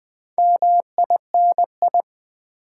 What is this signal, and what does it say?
Morse code, heard as a keyed tone, about 20 wpm, 700 Hz. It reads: MINI
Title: Build A Word - 4 Letters At A Time - 8-Letter Words - 20wpm